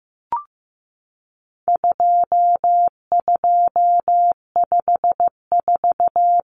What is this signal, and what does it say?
Morse code, heard as a keyed tone, about 15 wpm, 700 Hz. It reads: 2254
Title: Common Cut Numbers - 4-Digit 15wpm